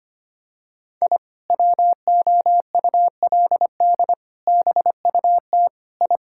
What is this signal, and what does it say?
Morse code, heard as a keyed tone, about 25 wpm, 700 Hz. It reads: I WOULD BUT I
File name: practice / 25wpm / Sentences with 2-Word Phrases - 25wpm